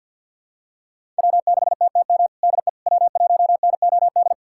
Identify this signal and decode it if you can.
Morse code, heard as a keyed tone, about 51 wpm, 700 Hz. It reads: W6TTM DE W1NOD
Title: USA Callsign DE Callsign - 5 Characters - 50wpm